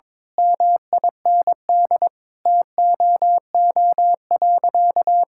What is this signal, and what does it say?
Morse code, heard as a keyed tone, about 22 wpm, 700 Hz. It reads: MIND TOO.
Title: Sentences from Top 1000 Words - Review Part 6 - 22wpm